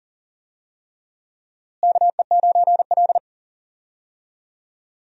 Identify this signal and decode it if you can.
Morse code, heard as a keyed tone, about 40 wpm, 700 Hz. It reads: KE9L
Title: USA Call Signs - 4 Characters - Repeated 3x - 40wpm